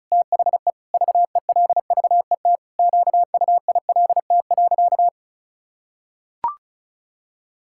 Morse code, heard as a keyed tone, about 36 wpm, 700 Hz. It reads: THE VELVET QUILT.
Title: Pangram Challenge v2 - 35wpm